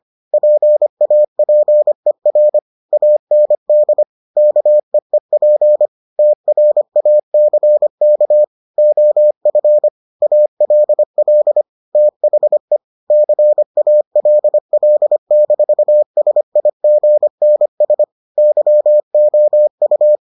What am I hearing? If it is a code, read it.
Morse code, heard as a keyed tone, about 25 wpm, 600 Hz. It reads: PAPER AND KEEP TRACK OF ALL THE CALL-SIGNS YOU